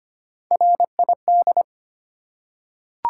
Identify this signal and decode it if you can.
Morse code, heard as a keyed tone, about 25 wpm, 700 Hz. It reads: RID
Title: Three Letter Words - 25wpm